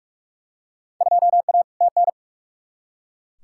Morse code, heard as a keyed tone, about 43 wpm, 700 Hz. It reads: 2A TN
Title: Field Day Exchanges - 45wpm